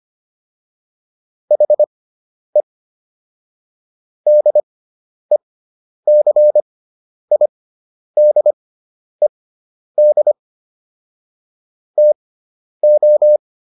Morse code, heard as a keyed tone, about 25 wpm, spaced as for 10 wpm, 600 Hz. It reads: HE DECIDED TO